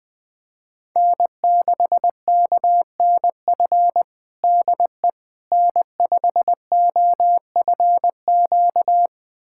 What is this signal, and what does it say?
Morse code, heard as a keyed tone, about 20 wpm, 700 Hz. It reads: N6KNF DE N5OFQ
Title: USA Callsign DE Callsign - 5 Characters - 20wpm